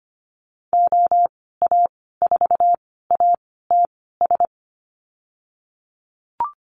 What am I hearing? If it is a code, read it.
Morse code, heard as a keyed tone, about 25 wpm, spaced as for 16 wpm, 700 Hz. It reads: OA4ATS
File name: Lesson 15g - Number 5 - Call Signs 25wpm